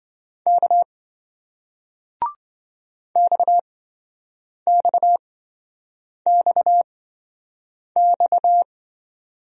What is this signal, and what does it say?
Morse code, heard as a keyed tone, about 27 wpm, spaced as for 7 wpm, 700 Hz. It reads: K XXXX